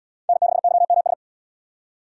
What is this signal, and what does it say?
Morse code, heard as a keyed tone, about 77 wpm, 700 Hz. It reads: N6YMS